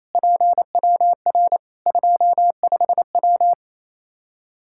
Morse code, heard as a keyed tone, about 28 wpm, 700 Hz. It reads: PWR 25W
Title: Practice QSO - Exchange 2 - Info - 28wpm